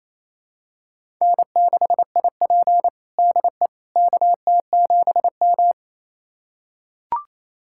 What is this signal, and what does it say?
Morse code, heard as a keyed tone, about 28 wpm, 700 Hz. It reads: N6IP DE KT7M
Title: USA Callsign DE Callsign - 4 Characters - 28wpm